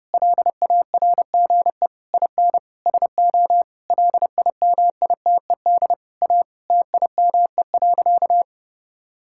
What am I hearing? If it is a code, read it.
Morse code, heard as a keyed tone, about 30 wpm, 700 Hz. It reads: LARGE IN SO LIMITED A TIME.
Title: Sentences from Top 1000 Words - Review Part 6 - 30wpm